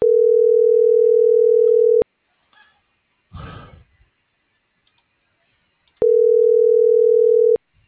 Ringback tone (North America), 440 + 480 Hz.